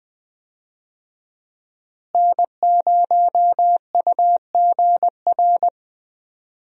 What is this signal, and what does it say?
Morse code, heard as a keyed tone, about 21 wpm, 700 Hz. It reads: N0UGR
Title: USA Call Signs - 5 Characters - Repeated 3x - 20wpm